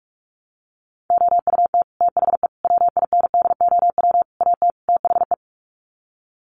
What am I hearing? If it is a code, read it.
Morse code, heard as a keyed tone, about 44 wpm, 700 Hz. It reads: OUT THE WINDOW AT THE
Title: Sets of 5 Words Encore2 45wpm